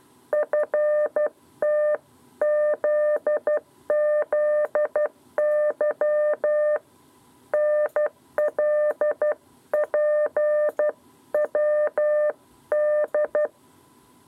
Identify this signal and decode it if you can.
Morse code, heard as a keyed tone, about 12 wpm, 600 Hz. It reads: FTZZY NLPWD
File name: 7b lezione cw 2